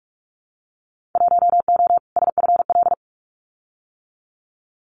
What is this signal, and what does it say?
Morse code, heard as a keyed tone, about 46 wpm, 700 Hz. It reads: 1O SFL
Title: Winter Field Day Exchanges 45wpm